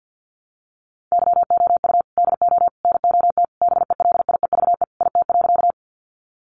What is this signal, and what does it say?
Morse code, heard as a keyed tone, about 50 wpm, 700 Hz. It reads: YOU DO NOT BELIEVE IT.
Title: Sentences from Top 1000 Words 50wpm